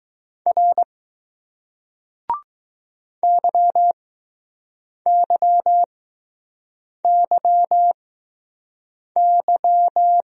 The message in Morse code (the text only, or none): R YYYY